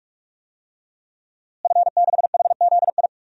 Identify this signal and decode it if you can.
Morse code, heard as a keyed tone, about 45 wpm, 700 Hz. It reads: F6HZI
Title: DXCC-100 Call Signs - 5 Characters - Repeated 3x - 45wpm